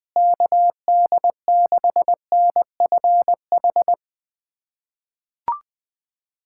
Morse code, heard as a keyed tone, about 20 wpm, 700 Hz. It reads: KD6NFH
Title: USA Callsign DE Callsign - 6 Characters - 20wpm